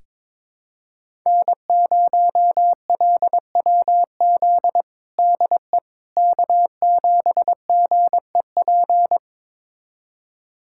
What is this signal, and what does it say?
Morse code, heard as a keyed tone, about 22 wpm, 700 Hz. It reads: N0LWZ DE K7GEP
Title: USA Callsign DE Callsign - 5 Characters - 22wpm